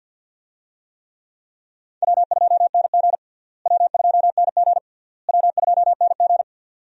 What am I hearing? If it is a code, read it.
Morse code, heard as a keyed tone, about 52 wpm, 700 Hz. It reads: W2NG W2NG W2NG
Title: USA Call Signs - 4 Characters - Repeated 3x - 50wpm